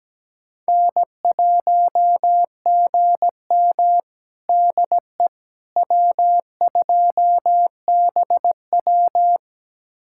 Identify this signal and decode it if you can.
Morse code, heard as a keyed tone, about 17 wpm, 700 Hz. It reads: N1GM DE W2BW
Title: USA Callsign DE Callsign - 4 Characters - 17wpm